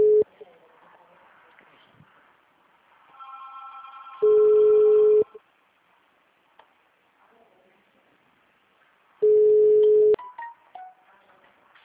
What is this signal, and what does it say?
Ringback tone (Europe, 425 Hz), with speech.